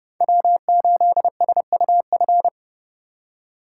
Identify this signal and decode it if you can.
Morse code, heard as a keyed tone, about 30 wpm, 700 Hz. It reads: W8SUF
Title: USA Call Signs - 5 Characters - 4x Character Spacing - 30wpm